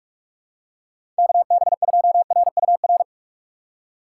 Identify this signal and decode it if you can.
Morse code, heard as a keyed tone, about 41 wpm, 700 Hz. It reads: KB2RUR
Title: USA Call Signs - 6 Characters - 2x Character Spacing - 45wpm